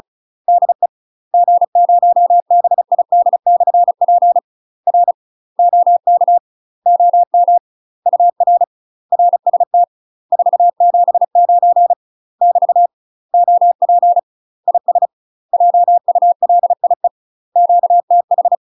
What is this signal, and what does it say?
Morse code, heard as a keyed tone, about 36 wpm, 700 Hz. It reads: DE G0BID/P R OK OM UR RST 479 <BT> OP IS JULIE QTH
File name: UK MASTER QSOs TRAINING / 35WPM / UK MASTER QSOs 0031